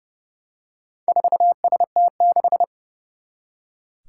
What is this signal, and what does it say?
Morse code, heard as a keyed tone, about 30 wpm, 700 Hz. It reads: 4ST6
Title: Four Challenging Letter-Number Combinations 30wpm